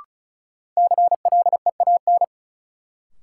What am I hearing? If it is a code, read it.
Morse code, heard as a keyed tone, about 34 wpm, 700 Hz. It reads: CLEAN
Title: Sentences from Top 1100 Words - New Word Drill - 35wpm